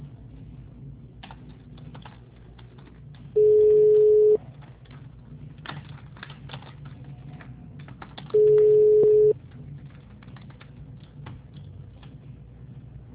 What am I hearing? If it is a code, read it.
Ringback tone (Europe, 425 Hz).